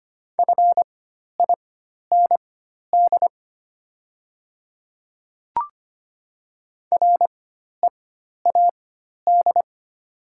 Morse code, heard as a keyed tone, about 25 wpm, spaced as for 12 wpm, 690 Hz. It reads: FIND READ
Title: 4x Character Spacing - Top 300 Words - 25wpm